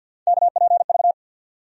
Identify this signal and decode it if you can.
Morse code, heard as a keyed tone, about 52 wpm, 700 Hz. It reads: KPV
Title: USA Call Sign Postfixes - 3 Letters 50wpm